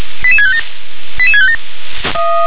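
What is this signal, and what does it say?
Five-tone selective call (ZVEI), heard as quick runs of five tones, tones of about 70 ms. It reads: 88657 88657
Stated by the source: ZVEI five-tone selective call 88658 received by SDR.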